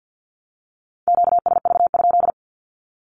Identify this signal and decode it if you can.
Morse code, heard as a keyed tone, about 49 wpm, 700 Hz. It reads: QSV?
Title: Q-Codes 50wpm